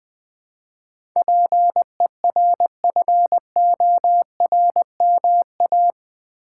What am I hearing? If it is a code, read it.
Morse code, heard as a keyed tone, about 20 wpm, 700 Hz. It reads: PERFORMA
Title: Build A Word - 4 Letters At A Time - 11-Letter Words - 20wpm